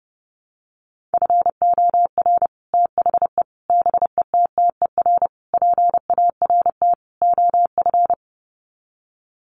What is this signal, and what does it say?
Morse code, heard as a keyed tone, about 28 wpm, 700 Hz. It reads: FOR THE BETTER PART OF
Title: Sets of 5 Words Encore2 30wpm